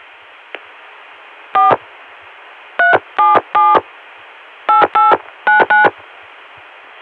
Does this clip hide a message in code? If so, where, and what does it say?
Yes: DTMF from 1.5 to 5.8 s, heard as telephone touch tones, tones of about 160 ms. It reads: 4 377 8899